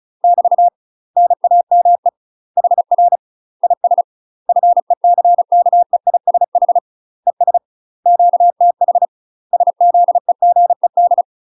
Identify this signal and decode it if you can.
Morse code, heard as a keyed tone, about 36 wpm, 700 Hz. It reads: <BT> NAME HR IS FECKEISH ES QTH SZEGED
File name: UK MASTER QSOs TRAINING / 35WPM / UK MASTER QSOs 0010